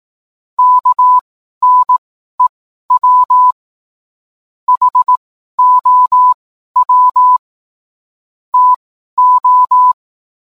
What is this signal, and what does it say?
Morse code, heard as a keyed tone, about 18 wpm, spaced as for 12 wpm, 1000 Hz. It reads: KNEW HOW TO
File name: cwblog / cw mp3 directory / mem morse 12